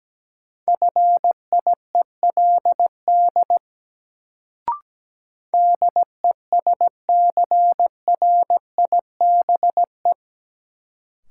Morse code, heard as a keyed tone, about 17 wpm, 700 Hz. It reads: FIELD DESCRIBE
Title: Sentences from Top 1000 Words - New Word Drill - 17wpm